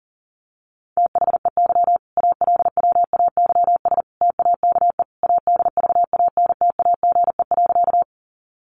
Morse code, heard as a keyed tone, about 40 wpm, 700 Hz. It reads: THEY ALWAYS TAKE ADVANTAGE.
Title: Sentences from Top 900 Words 40wpm